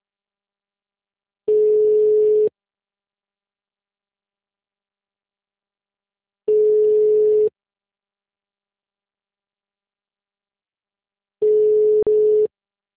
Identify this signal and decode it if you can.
Ringback tone (Europe, 425 Hz).
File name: out- -2028- - - .27214